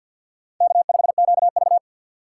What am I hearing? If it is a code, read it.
Morse code, heard as a keyed tone, about 53 wpm, 690 Hz. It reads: K5QV
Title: USA Call Signs - 4 Characters - 6x Character Spacing - 50wpm